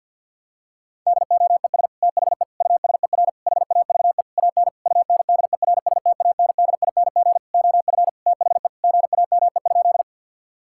Morse code, heard as a keyed tone, about 50 wpm, 700 Hz. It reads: DOES THE USER HAVE AN UNDERSTANDING OF THE GAME?
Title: Sentences from Top 1000 Words - Review Part 5 - 50wpm